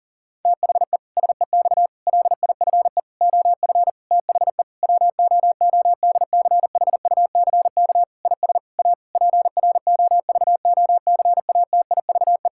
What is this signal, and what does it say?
Morse code, heard as a keyed tone, about 40 wpm, 700 Hz. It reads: THE SEX LIFE OF THE WOODCHUCK IS A PROVOCATIVE
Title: Pangram Challenge v2 - 40wpm